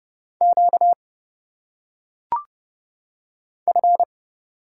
Morse code, heard as a keyed tone, about 30 wpm, 700 Hz. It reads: Q F
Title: Single Letters - Phonetic Pronunciation 30wpm